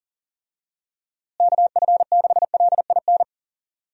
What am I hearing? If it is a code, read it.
Morse code, heard as a keyed tone, about 37 wpm, 700 Hz. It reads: KF6LIN
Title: USA Call Signs - 6 Characters - Repeated 2x - 40wpm